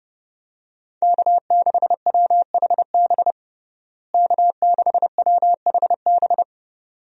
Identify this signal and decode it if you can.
Morse code, heard as a keyed tone, about 31 wpm, 700 Hz. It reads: K6WHB K6WHB
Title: USA Call Signs - 5 Characters - Repeated 2x - 30wpm